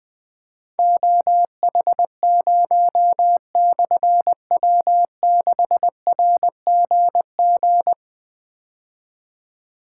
Morse code, heard as a keyed tone, about 20 wpm, 700 Hz. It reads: OH0/W6RGG
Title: Difficult Call Signs 20wpm